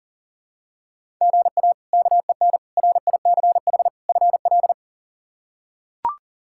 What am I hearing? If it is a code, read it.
Morse code, heard as a keyed tone, about 40 wpm, 700 Hz. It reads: GA KEN RICH FL
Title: K1USN SST QSO Practice - 40wpm